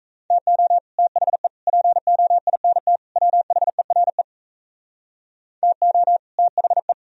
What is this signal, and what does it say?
Morse code, heard as a keyed tone, about 35 wpm, 700 Hz. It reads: TO THE POINT WHERE TO THE